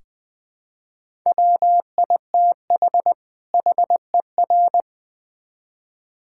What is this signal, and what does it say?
Morse code, heard as a keyed tone, about 21 wpm, 700 Hz. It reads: WITH HER
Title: Build A Phrase - 2-Word Phrases - 20wpm